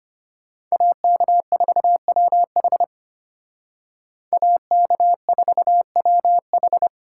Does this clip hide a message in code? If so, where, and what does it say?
Yes: Morse code from 0.7 to 6.9 s, heard as a keyed tone, about 33 wpm, 700 Hz. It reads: AK4WH AK4WH